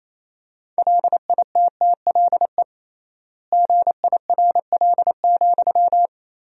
Morse code, heard as a keyed tone, about 28 wpm, 700 Hz. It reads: LITTLE GIRL,
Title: 3x Word-Spacing - Sentences from Top 400 Words - 28wpm